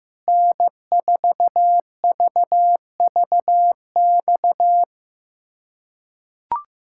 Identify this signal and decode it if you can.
Morse code, heard as a keyed tone, about 15 wpm, 700 Hz. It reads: N4VVX
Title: USA Callsign DE Callsign 15wpm